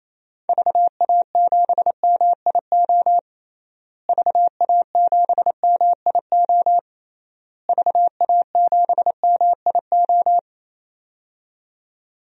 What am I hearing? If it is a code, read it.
Morse code, heard as a keyed tone, about 27 wpm, 700 Hz. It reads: VA7MIO VA7MIO VA7MIO
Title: DXCC-100 Call Signs - 6 Characters - Repeated 3x - 28wpm